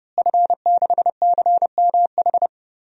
Morse code, heard as a keyed tone, about 32 wpm, 700 Hz. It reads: F6CMH
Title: DXCC-100 Call Signs - 5 Characters - Repeated 3x - 30wpm